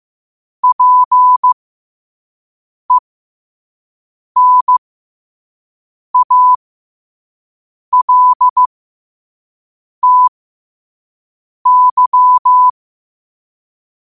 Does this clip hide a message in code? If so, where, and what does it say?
Yes: Morse code from 0.6 to 12.7 s, heard as a keyed tone, about 15 wpm, spaced as for 5 wpm, 1000 Hz. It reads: PENALTY